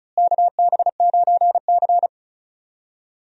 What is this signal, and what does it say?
Morse code, heard as a keyed tone, about 36 wpm, 700 Hz. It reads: KB9C
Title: USA Call Signs - 4 Characters - Repeated 3x - 35wpm